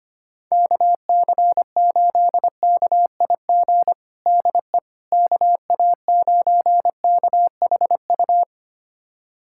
Morse code, heard as a keyed tone, about 25 wpm, 700 Hz. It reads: KC8KIG DE KA9KHU
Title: USA Callsign DE Callsign - 6 Characters - 25wpm